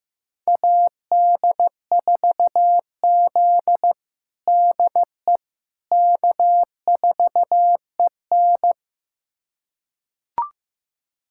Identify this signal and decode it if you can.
Morse code, heard as a keyed tone, about 15 wpm, 700 Hz. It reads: AD4Z DE K4EN
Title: USA Callsign DE Callsign - 4 Characters - 15wpm